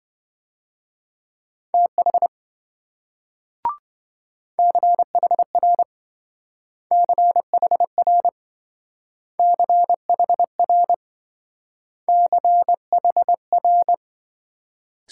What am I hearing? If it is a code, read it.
Morse code, heard as a keyed tone, about 26 wpm, 700 Hz. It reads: TH CHR CHR CHR CHR